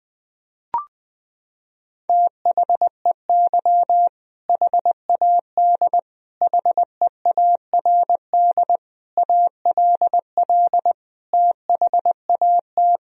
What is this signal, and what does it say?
Morse code, heard as a keyed tone, about 21 wpm, 700 Hz. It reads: THEY HAD HEARD ALL THAT
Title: Sentences from Top 400 Words and Abbreviations 20wpm